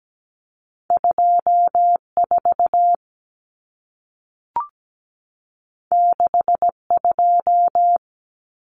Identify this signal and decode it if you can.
Morse code, heard as a keyed tone, about 17 wpm, 700 Hz. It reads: 24 62
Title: Common Cut Numbers - 2-Digit 17wpm